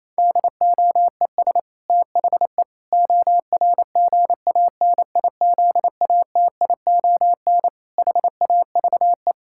Morse code, heard as a keyed tone, about 27 wpm, 700 Hz. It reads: DOES THE ORGANIZATION HAVE
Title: Sentences from Top 1000 Words - Review Part 6 - 28wpm